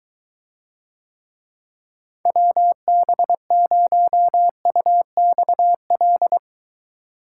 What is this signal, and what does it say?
Morse code, heard as a keyed tone, about 23 wpm, 700 Hz. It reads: WB0UXL